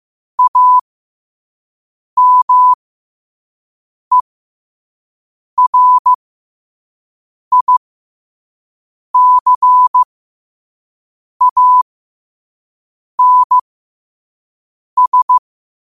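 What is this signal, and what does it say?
Morse code, heard as a keyed tone, about 15 wpm, spaced as for 5 wpm, 1000 Hz. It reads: AMERICANS
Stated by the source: Morse code at 5 wpm.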